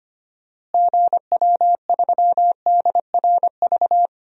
Morse code, heard as a keyed tone, about 25 wpm, 700 Hz. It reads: GW3DRV
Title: DXCC-100 Call Signs - 6 Characters - Repeated 3x - 25wpm